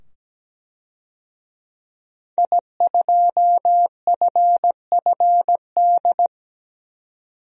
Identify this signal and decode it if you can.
Morse code, heard as a keyed tone, about 17 wpm, 700 Hz. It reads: I2FFD
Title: DXCC-100 Call Signs - 5 Characters - 17wpm